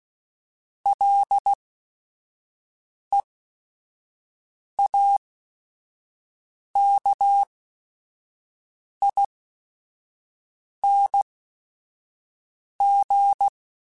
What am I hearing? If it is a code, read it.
Morse code, heard as a keyed tone, about 16 wpm, spaced as for 5 wpm, 800 Hz. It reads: LEAKING
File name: Special INGs / Special INGs-mp3s / Sngl Ing @5